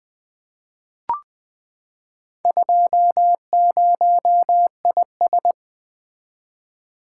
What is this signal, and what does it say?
Morse code, heard as a keyed tone, about 20 wpm, 700 Hz. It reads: 20IS
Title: Four Letter-Number Combinations - 20wpm